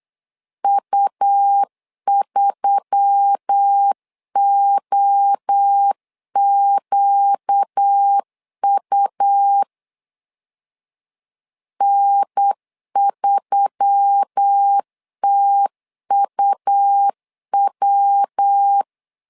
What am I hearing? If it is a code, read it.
Morse code, heard as a keyed tone, about 8 wpm, 800 Hz. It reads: U3OQU N3TUW
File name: LEZIONI / 4 Q1S3 8wpm 3037